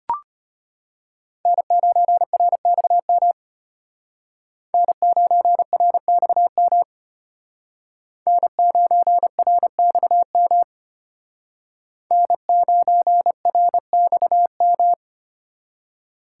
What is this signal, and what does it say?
Morse code, heard as a keyed tone, about 32 wpm, 700 Hz. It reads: N9RXM N9RXM N9RXM N9RXM